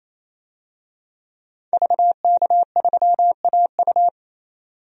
Morse code, heard as a keyed tone, about 28 wpm, 700 Hz. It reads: VK3AU
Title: DXCC-100 Call Signs - 5 Characters - 28wpm